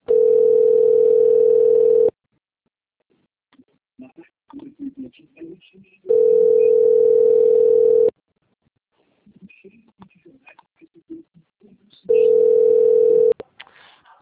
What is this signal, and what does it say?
Ringback tone (North America), 440 + 480 Hz, with speech.
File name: rg-1600-unknown- - - .13405